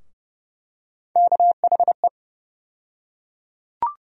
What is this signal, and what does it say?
Morse code, heard as a keyed tone, about 30 wpm, 700 Hz. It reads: KHE
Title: USA Call Sign Postfixes - 3 Letters 30wpm